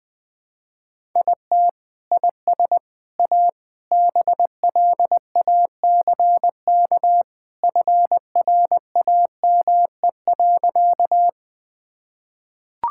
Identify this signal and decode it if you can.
Morse code, heard as a keyed tone, about 21 wpm, 700 Hz. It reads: IT IS A BLACK FRAME.
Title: Sentences from Top 1000 Words - Review Part 6 - 20wpm